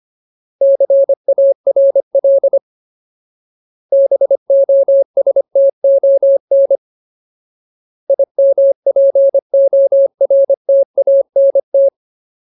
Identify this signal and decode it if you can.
Morse code, heard as a keyed tone, about 25 wpm, 550 Hz. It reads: CARL BOSTON IMPORTANT
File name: Ultimate Word List / 25WPM Ultimate